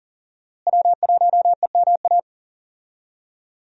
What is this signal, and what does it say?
Morse code, heard as a keyed tone, about 39 wpm, 700 Hz. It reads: W1EMA
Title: USA Call Signs - 5 Characters - Repeated 3x - 40wpm